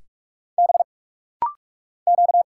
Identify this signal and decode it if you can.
Morse code, heard as a keyed tone, about 46 wpm, 700 Hz. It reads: B Q
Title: Single Letters - Rapid-Fire 45wpm